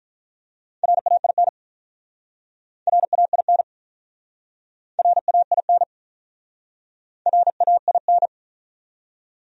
Morse code, heard as a keyed tone, about 43 wpm, 700 Hz. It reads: RAIN RAIN RAIN RAIN
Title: Speed-Racing Top 500 Words 35wpm